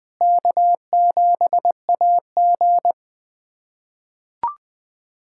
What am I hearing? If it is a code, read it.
Morse code, heard as a keyed tone, about 20 wpm, 700 Hz. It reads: K7AG
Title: USA Callsign DE Callsign - 4 Characters - 20wpm